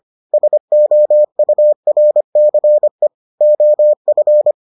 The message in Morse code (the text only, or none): SOURCE OF